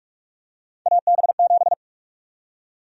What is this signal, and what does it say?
Morse code, heard as a keyed tone, about 46 wpm, 700 Hz. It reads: AB7
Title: USA Call Sign Prefixes - 45wpm